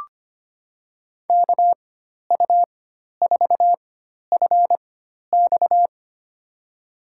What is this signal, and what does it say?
Morse code, heard as a keyed tone, about 25 wpm, spaced as for 12 wpm, 700 Hz. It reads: KU4FX
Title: USA Call Signs - 5 Characters - 4x Character Spacing - 25wpm